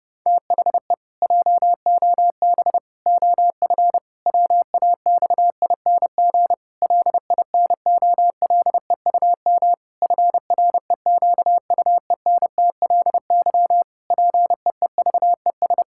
Morse code, heard as a keyed tone, about 30 wpm, 700 Hz. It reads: THE JOB OF WAXING LINOLEUM FREQUENTLY PEEVES
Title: Pangram Challenge v2 - 30wpm